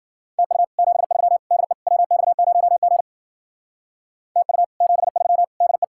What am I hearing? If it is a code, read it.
Morse code, heard as a keyed tone, about 51 wpm, 700 Hz. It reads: TU 73 DE WX0G TU 73 DE